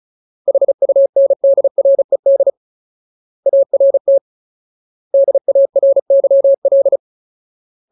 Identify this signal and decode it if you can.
Morse code, heard as a keyed tone, about 36 wpm, 550 Hz. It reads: HUNDRED ART DARYL
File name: Ultimate Word List / 35WPM Ultimate